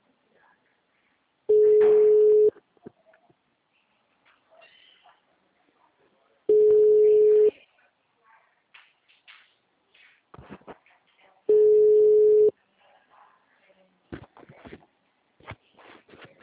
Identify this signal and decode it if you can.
Ringback tone (Europe, 425 Hz), with speech.